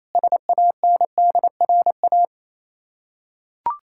Morse code, heard as a keyed tone, about 27 wpm, 700 Hz. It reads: SANDRA
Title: Practice QSO - Exchange 1 - Beginning - 28wpm